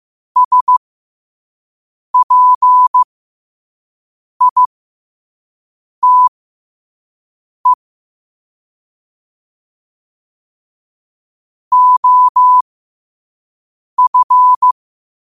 Morse code, heard as a keyed tone, about 15 wpm, spaced as for 5 wpm, 1000 Hz. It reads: SPITE OF
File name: 5wpm